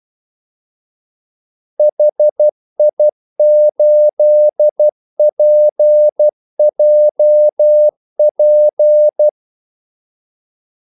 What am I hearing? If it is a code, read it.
Morse code, heard as a keyed tone, about 12 wpm, 600 Hz. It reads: HI8PJP
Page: 2 ekstra ordmellemrum mellem kaldesignalerne